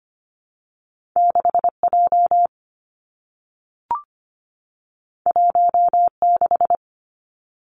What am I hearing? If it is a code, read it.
Morse code, heard as a keyed tone, about 25 wpm, 700 Hz. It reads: 6J 16
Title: Two Challenging Letter-Number Combinations 25wpm